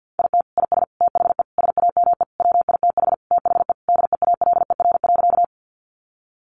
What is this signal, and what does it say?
Morse code, heard as a keyed tone, about 50 wpm, 700 Hz. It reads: IT IS THE SAME WITH THE DEALER.
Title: Sentences from Top 1100 Words 50wpm